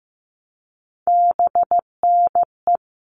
Morse code, heard as a keyed tone, about 15 wpm, 700 Hz. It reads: BNE
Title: All Cut Numbers - 3-Digit 15wpm